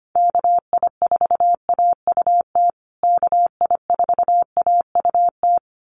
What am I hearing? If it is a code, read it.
Morse code, heard as a keyed tone, about 25 wpm, 700 Hz. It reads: KI4AUT KI4AUT
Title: USA Call Signs Repeated 2x 25wpm